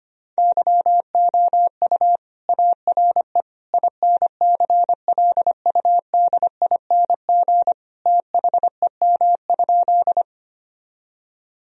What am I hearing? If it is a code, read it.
Morse code, heard as a keyed tone, about 25 wpm, 700 Hz. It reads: YOU ARE INCLUDING THEM?
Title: Sentences from Top 1000 Words - Review Part 4 - 25wpm